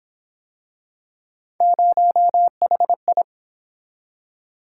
Morse code, heard as a keyed tone, about 26 wpm, 700 Hz. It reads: 0HI
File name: Speed-Racing - Three Letter-Number Combinations - 22wpm